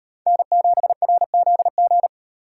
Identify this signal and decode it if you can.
Morse code, heard as a keyed tone, about 38 wpm, 700 Hz. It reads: N7RZG